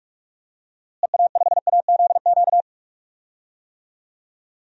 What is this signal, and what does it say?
Morse code, heard as a keyed tone, about 46 wpm, 700 Hz. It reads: EA5AZQ
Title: DXCC-100 Call Signs - 6 Characters - Repeated 3x - 45wpm